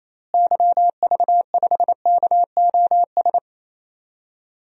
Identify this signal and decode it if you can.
Morse code, heard as a keyed tone, about 28 wpm, 700 Hz. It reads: YV5KOS
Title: DXCC-100 Call Signs - 6 Characters - Repeated 2x - 28wpm